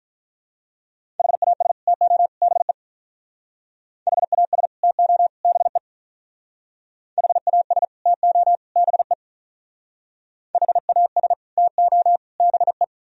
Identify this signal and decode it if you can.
Morse code, heard as a keyed tone, about 44 wpm, 700 Hz. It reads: HAS TO BE HAS TO BE HAS TO BE HAS TO BE